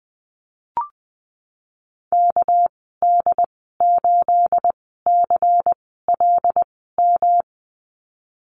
Morse code, heard as a keyed tone, about 20 wpm, spaced as for 15 wpm, 700 Hz. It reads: KD8CLM
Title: 231 Lesson 37g - Letter x - Call Signs 20wpm